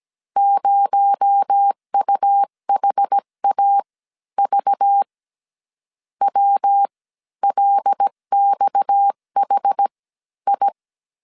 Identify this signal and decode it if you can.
Morse code, heard as a keyed tone, about 17 wpm, spaced as for 13 wpm, 800 Hz. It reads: 0UHAV WLXHI
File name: CW / DET / 1 6 1500C S C 16wpm 2733